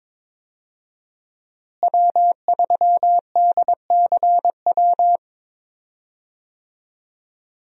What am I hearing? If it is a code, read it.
Morse code, heard as a keyed tone, about 22 wpm, 700 Hz. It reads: W3DCW
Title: USA Call Signs - 5 Characters - Repeated 3x - 22wpm